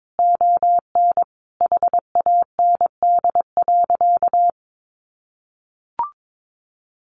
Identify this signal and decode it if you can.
Morse code, heard as a keyed tone, about 22 wpm, 700 Hz. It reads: ON HAND.
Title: Sentences from Top 1500 Words 22wpm